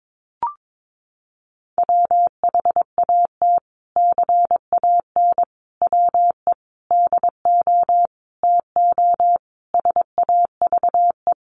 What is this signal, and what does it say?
Morse code, heard as a keyed tone, about 22 wpm, 700 Hz. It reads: WHAT CAN WE DO TO SAVE
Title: Sentences from Top 1100 Words 22wpm